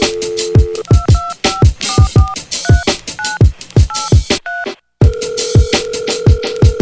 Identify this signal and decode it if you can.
DTMF, heard as telephone touch tones, tones of about 200 ms. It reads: 225443983